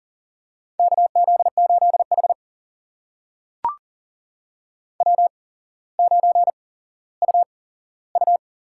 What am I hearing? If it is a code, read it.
Morse code, heard as a keyed tone, about 39 wpm, spaced as for 17 wpm, 690 Hz. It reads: KZ8H W9UU
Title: USA Call Signs - 4 Characters - 8x Character Spacing - 40wpm